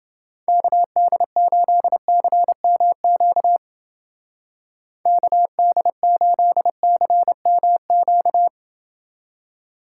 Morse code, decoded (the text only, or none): KD8CMQ KD8CMQ